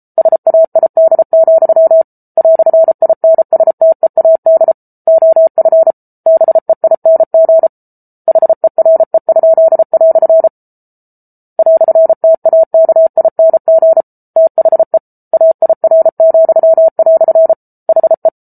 Morse code, heard as a keyed tone, about 34 wpm, 650 Hz. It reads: SAID, "INSTEAD OF BEING HERE?" "TAKING THE AIR," HE